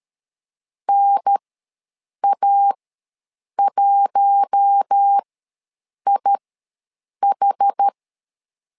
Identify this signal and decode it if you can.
Morse code, heard as a keyed tone, about 13 wpm, spaced as for 7 wpm, 800 Hz. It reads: NA1IH
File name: LEZIONI / 4 Q1 ws 12wpm 2129